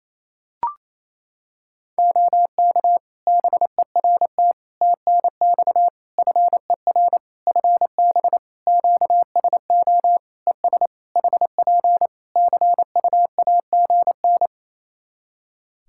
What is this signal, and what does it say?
Morse code, heard as a keyed tone, about 28 wpm, 700 Hz. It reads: OK BERT TNX FER FB QSO ES HP CUAGN
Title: Practice QSO - Exchange 4 - Ending - 28wpm